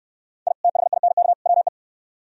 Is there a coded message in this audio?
Yes: Morse code from 0.5 to 1.7 s, heard as a keyed tone, about 64 wpm, 700 Hz. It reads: I THINK WE